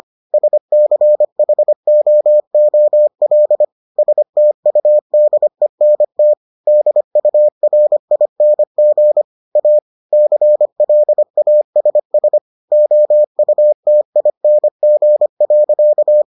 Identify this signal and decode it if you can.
Morse code, heard as a keyed tone, about 25 wpm, 600 Hz. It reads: SCHOOL STUDENT DURING A CLASS OUTING.